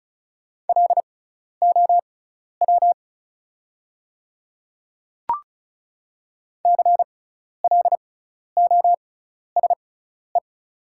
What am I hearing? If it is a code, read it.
Morse code, heard as a keyed tone, about 36 wpm, spaced as for 12 wpm, 690 Hz. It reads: LOW CLOSE
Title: 6x Character Spacing - Top 300 Words - 35wpm